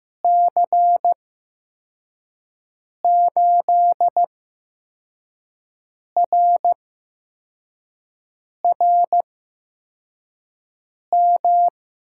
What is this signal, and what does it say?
Morse code, heard as a keyed tone, about 15 wpm, spaced as for 4 wpm, 700 Hz. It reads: C8RRM
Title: USA Call Signs - 6 Characters - 8x Character Spacing - 15wpm